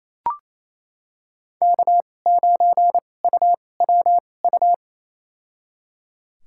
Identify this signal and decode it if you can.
Morse code, heard as a keyed tone, about 28 wpm, spaced as for 20 wpm, 700 Hz. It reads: K9UWU
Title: USA Call Signs - 5 Characters - 2x Character Spacing - 28wpm